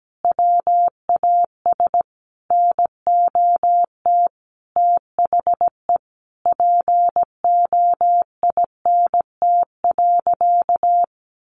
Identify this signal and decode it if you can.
Morse code, heard as a keyed tone, about 17 wpm, 700 Hz. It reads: WAS NOT THE POINT.
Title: Sentences from Top 600 Words 17wpm